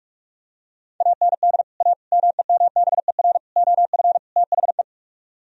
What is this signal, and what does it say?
Morse code, heard as a keyed tone, about 46 wpm, 700 Hz. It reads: AND A MEMBER OF THE